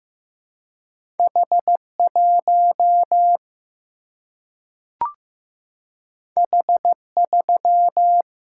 Morse code, heard as a keyed tone, about 15 wpm, 700 Hz. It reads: H 1 H 3
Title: Single-Letter Single-Number 15wpm